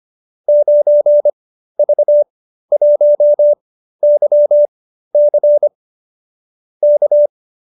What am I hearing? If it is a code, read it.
Morse code, heard as a keyed tone, about 25 wpm, spaced as for 13 wpm, 590 Hz. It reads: 9V1YC K